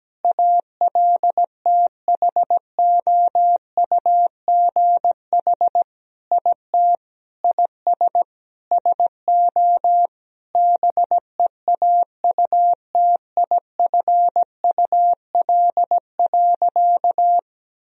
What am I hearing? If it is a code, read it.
Morse code, heard as a keyed tone, about 17 wpm, 700 Hz. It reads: ALTHOUGH IT IS SO BEAUTIFUL.
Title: Sentences from Top 1000 Words - Review Part 3 - 17wpm